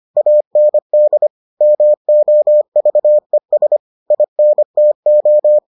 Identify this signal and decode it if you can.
Morse code, heard as a keyed tone, about 25 wpm, 600 Hz. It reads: AND MOVES INTO